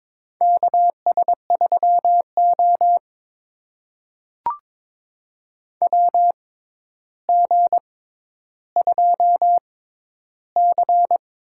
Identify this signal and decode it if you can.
Morse code, heard as a keyed tone, about 21 wpm, spaced as for 11 wpm, 700 Hz. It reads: KS3O WG2C
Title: USA Call Signs - 4 Characters - 6x Character Spacing - 22wpm